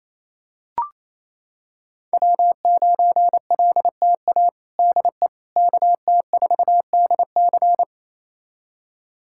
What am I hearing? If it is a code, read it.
Morse code, heard as a keyed tone, about 28 wpm, 690 Hz. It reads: W9LTA DE KT4DC
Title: USA Callsign DE Callsign - 5 Characters - 28wpm